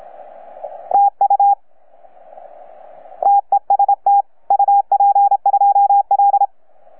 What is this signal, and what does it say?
Morse code, heard as a keyed tone, about 28 wpm, 810 Hz. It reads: TU TEST UP2L
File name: UP2L-20-CW